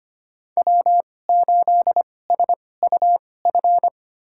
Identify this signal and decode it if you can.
Morse code, heard as a keyed tone, about 25 wpm, spaced as for 18 wpm, 700 Hz. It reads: W8SUF
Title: USA Call Signs - 5 Characters - 2x Character Spacing - 25wpm